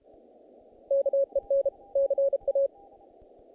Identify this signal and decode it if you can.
Morse code, heard as a keyed tone, about 32 wpm, 550 Hz. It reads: KEN CA
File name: contests / naqp-cw-k5tr-op-k5pi- -1800